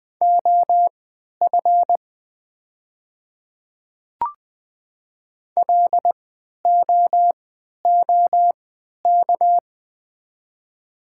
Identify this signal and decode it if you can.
Morse code, heard as a keyed tone, about 20 wpm, spaced as for 11 wpm, 700 Hz. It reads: OF LOOK
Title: 3x Character Spacing - Top 100 Words - 20wpm